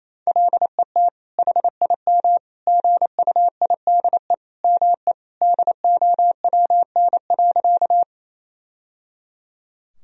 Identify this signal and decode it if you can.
Morse code, heard as a keyed tone, about 28 wpm, 700 Hz. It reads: LET HIM GUIDE ME DOWN.
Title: Sentences from Top 1000 Words - Review Part 3 - 28wpm